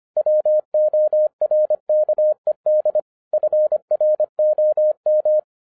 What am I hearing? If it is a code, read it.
Morse code, heard as a keyed tone, about 26 wpm, 610 Hz. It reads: WORKED FROM
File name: 2012-01-mp3 / n14-FUN WITH NUMBERS-25wpm